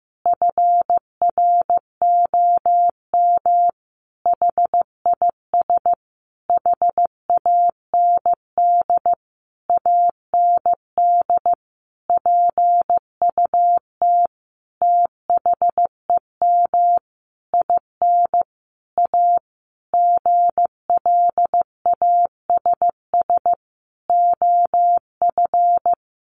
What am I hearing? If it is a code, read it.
Morse code, heard as a keyed tone, about 15 wpm, 700 Hz. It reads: FROM HIS HAND AND PUT THEM IN A GLASS OF
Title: Sentences from Top 1300 Words 15wpm